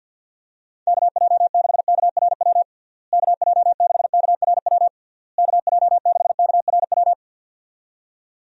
Morse code, heard as a keyed tone, about 50 wpm, 700 Hz. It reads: KJ6KRW KJ6KRW KJ6KRW
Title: USA Call Signs - 6 Characters - Repeated 3x - 50wpm